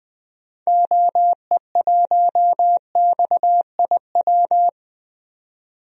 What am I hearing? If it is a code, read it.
Morse code, heard as a keyed tone, about 20 wpm, 700 Hz. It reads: OE1XIW
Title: DXCC-100 Call Signs - 6 Characters - 20wpm